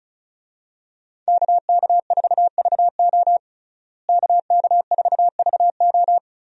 Morse code, heard as a keyed tone, about 34 wpm, 700 Hz. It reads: KK4VO KK4VO
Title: USA Call Signs - 5 Characters - Repeated 2x - 35wpm